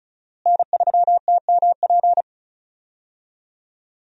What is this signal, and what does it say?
Morse code, heard as a keyed tone, about 36 wpm, 700 Hz. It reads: N3TMP
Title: USA Call Signs - 5 Characters - Repeated 2x - 35wpm